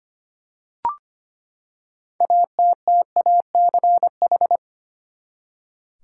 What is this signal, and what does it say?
Morse code, heard as a keyed tone, about 25 wpm, 700 Hz. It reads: ATTACH
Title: Six Letter Words - 25wpm